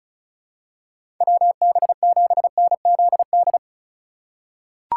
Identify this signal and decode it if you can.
Morse code, heard as a keyed tone, about 36 wpm, 700 Hz. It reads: WB7NZD
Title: USA Call Signs - 6 Characters - 3x Character Spacing - 35wpm